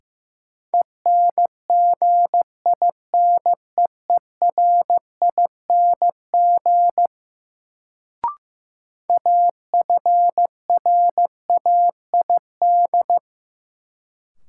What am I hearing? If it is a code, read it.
Morse code, heard as a keyed tone, about 15 wpm, 700 Hz. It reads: ENGINEERING AFRAID
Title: Sentences from Top 1500 Words - New Word Drill - 15wpm